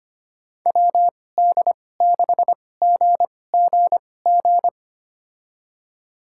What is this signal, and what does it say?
Morse code, heard as a keyed tone, about 25 wpm, spaced as for 18 wpm, 700 Hz. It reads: WD6GGG
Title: USA Call Signs - 6 Characters - 2x Character Spacing - 25wpm